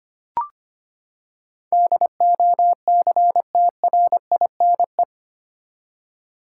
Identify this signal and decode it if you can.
Morse code, heard as a keyed tone, about 25 wpm, 700 Hz. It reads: DOCTRINE
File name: Eight Letter Words - 25wpm